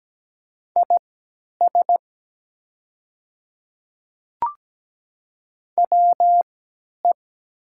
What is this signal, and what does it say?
Morse code, heard as a keyed tone, about 17 wpm, spaced as for 10 wpm, 700 Hz. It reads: IS WE
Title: 3x Character Spacing - Top 100 Words - 17wpm